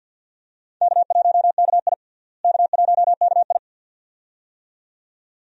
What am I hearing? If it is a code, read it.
Morse code, heard as a keyed tone, about 50 wpm, 700 Hz. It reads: K1KI K1KI
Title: USA Call Signs - 4 Characters - Repeated 2x - 50wpm